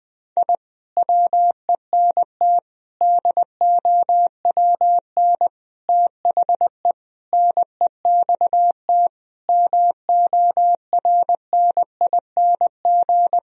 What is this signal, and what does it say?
Morse code, heard as a keyed tone, about 20 wpm, 700 Hz. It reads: I WENT DOWN THE NEXT MORNING
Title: Sentences from Top 1000 Words - Review Part 1 - 20wpm